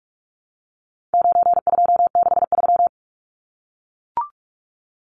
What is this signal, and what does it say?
Morse code, heard as a keyed tone, about 44 wpm, 700 Hz. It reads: 9263
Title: Numbers - 4-Digit 45wpm